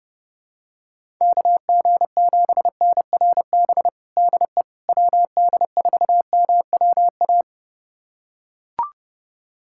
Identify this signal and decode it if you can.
Morse code, heard as a keyed tone, about 30 wpm, 700 Hz. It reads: KG7NRB DE WD4MWA
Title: USA Callsign DE Callsign 30wpm